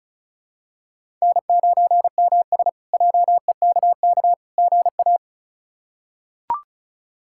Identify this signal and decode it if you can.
Morse code, heard as a keyed tone, about 36 wpm, 700 Hz. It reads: N9MS JEKK GA
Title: K1USN SST QSO Practice - 35wpm